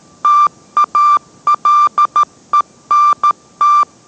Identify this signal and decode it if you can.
Morse code, heard as a keyed tone, about 13 wpm, 1180 Hz. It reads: TALENT